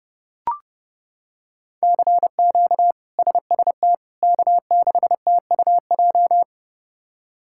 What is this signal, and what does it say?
Morse code, heard as a keyed tone, about 32 wpm, 700 Hz. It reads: CQ SST K6TUJ
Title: K1USN SST QSO Practice - 30wpm